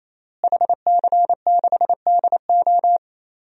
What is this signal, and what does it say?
Morse code, heard as a keyed tone, about 27 wpm, 700 Hz. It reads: HC6DO
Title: DXCC-100 Call Signs - 5 Characters - Repeated 3x - 28wpm